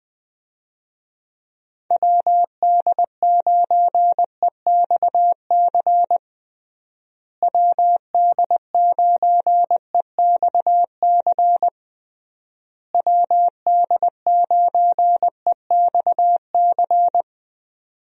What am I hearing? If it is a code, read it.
Morse code, heard as a keyed tone, about 20 wpm, 700 Hz. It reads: WD9EXC WD9EXC WD9EXC
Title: USA Call Signs - 6 Characters - Repeated 3x - 20wpm